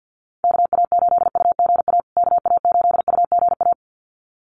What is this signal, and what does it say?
Morse code, heard as a keyed tone, about 50 wpm, 700 Hz. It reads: KA8UGA KA8UGA
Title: USA Call Signs Repeated 2x 50wpm